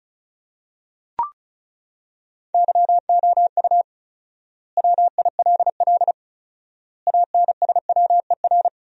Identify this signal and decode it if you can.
Morse code, heard as a keyed tone, about 34 wpm, 700 Hz. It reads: YOU WILL ANSWER
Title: 4x Word-Spacing - Sentences from Top 300 Words - 35wpm